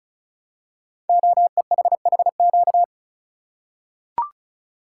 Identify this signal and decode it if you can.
Morse code, heard as a keyed tone, about 36 wpm, 700 Hz. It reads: OEHHQ
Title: Code Groups - 5 Letters - 35wpm